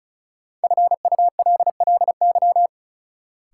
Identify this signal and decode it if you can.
Morse code, heard as a keyed tone, about 34 wpm, 700 Hz. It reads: FULLY
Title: Sentences from Top 1300 Words - New Word Drill Alt - 35wpm